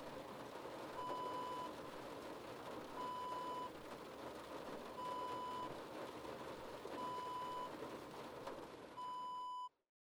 There is no code.